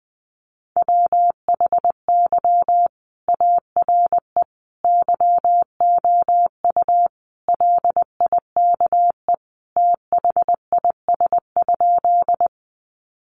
Morse code, heard as a keyed tone, about 19 wpm, 700 Hz. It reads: WHY ARE YOU LIKE THIS?
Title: Sentences from Top 200 Words 20wpm